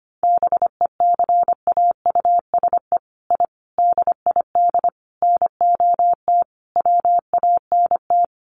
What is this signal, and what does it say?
Morse code, heard as a keyed tone, about 25 wpm, 700 Hz. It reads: BECAUSE I DID NOT WANT
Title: Sets of 5 Words Encore2 25wpm